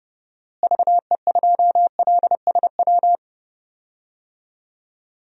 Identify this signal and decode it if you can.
Morse code, heard as a keyed tone, about 30 wpm, 700 Hz. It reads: VE2LSW
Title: DXCC-100 Call Signs - 6 Characters - Repeated 2x - 30wpm